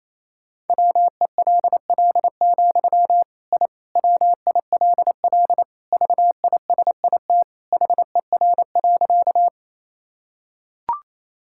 Morse code, heard as a keyed tone, about 28 wpm, 700 Hz. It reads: WELL, I WILL VISIT HER.
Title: Sentences from Top 1000 Words - Review Part 6 - 28wpm